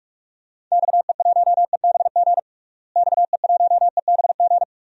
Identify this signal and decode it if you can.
Morse code, heard as a keyed tone, about 41 wpm, 700 Hz. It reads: XE1EBG XE1EBG
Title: DXCC-100 Call Signs - 6 Characters - Repeated 2x - 45wpm